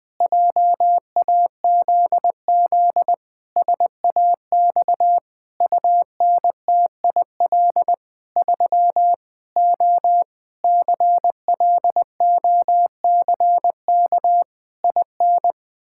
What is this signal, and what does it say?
Morse code, heard as a keyed tone, about 20 wpm, 700 Hz. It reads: JAZZ SAX UNTIL 3 O CLOCK IN
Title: Pangram Challenge v2 - 20wpm